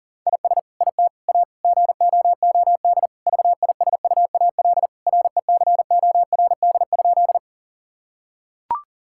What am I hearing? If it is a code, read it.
Morse code, heard as a keyed tone, about 40 wpm, 700 Hz. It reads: IS IT A GOOD VISUAL RECORD?
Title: Sentences from Top 1000 Words - Review Part 6 - 40wpm